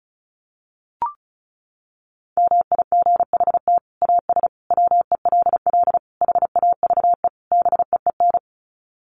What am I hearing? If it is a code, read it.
Morse code, heard as a keyed tone, about 34 wpm, 700 Hz. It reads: MIGHT AS WELL HAVE BEEN
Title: Sets of 5 Words Encore2 35wpm